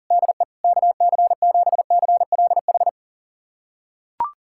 Morse code, heard as a keyed tone, about 40 wpm, 700 Hz. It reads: DE KC7CLH
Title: USA Callsign DE Callsign 40wpm